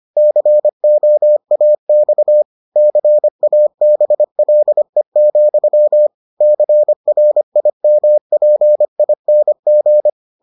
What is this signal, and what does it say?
Morse code, heard as a keyed tone, about 25 wpm, 600 Hz. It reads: COAX CABLE, CRIMPING